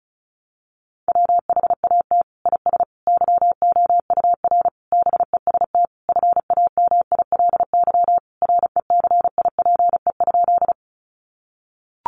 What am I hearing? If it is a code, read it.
Morse code, heard as a keyed tone, about 34 wpm, 700 Hz. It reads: WHAT IS YOUR BEST FAMILY RECIPE?
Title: Sentences from Top 1200 Words 35wpm